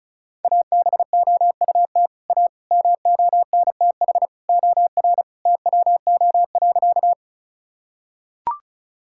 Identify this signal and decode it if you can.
Morse code, heard as a keyed tone, about 34 wpm, 700 Hz. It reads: ABOUT A MONTH OR TWO.
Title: Sentences from Top 1000 Words - Review Part 5 - 35wpm